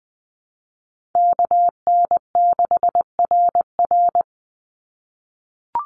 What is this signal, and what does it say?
Morse code, heard as a keyed tone, about 19 wpm, 700 Hz. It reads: KN6RR
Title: USA Call Signs 20wpm